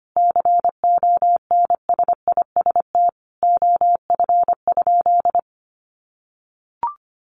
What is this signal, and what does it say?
Morse code, heard as a keyed tone, about 25 wpm, 700 Hz. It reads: CONSIST OF?
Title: Sentences from Top 1400 Words 25wpm